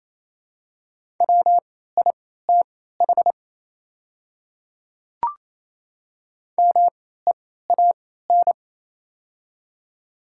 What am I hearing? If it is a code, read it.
Morse code, heard as a keyed tone, about 28 wpm, spaced as for 16 wpm, 700 Hz. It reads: WITH MEAN
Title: 3x Character Spacing - Top 200 Words - 28wpm